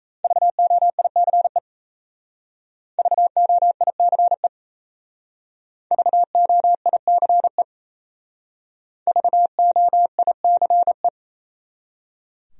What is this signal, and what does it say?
Morse code, heard as a keyed tone, about 36 wpm, 700 Hz. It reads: VOICE VOICE VOICE VOICE